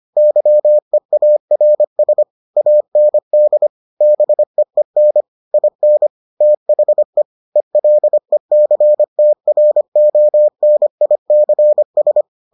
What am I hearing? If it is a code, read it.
Morse code, heard as a keyed tone, about 25 wpm, 600 Hz. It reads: YEARS AND BEEN IN THE ELECTRONICS